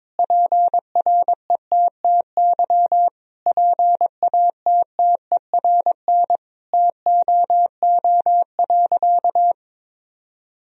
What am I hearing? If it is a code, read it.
Morse code, heard as a keyed tone, about 22 wpm, 700 Hz. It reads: PRETTY PATTERN TOO.
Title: Sentences from Top 1000 Words - Review Part 2 - 22wpm